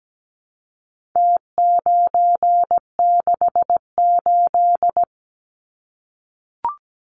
Morse code, heard as a keyed tone, about 17 wpm, 700 Hz. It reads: T968
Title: Numbers - Contest Style - 4-Digit 17wpm